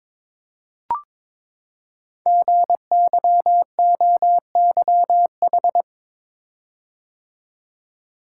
Morse code, heard as a keyed tone, about 22 wpm, 700 Hz. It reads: GYOYH
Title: Code Groups - 5 Letter:Numbers - 22wpm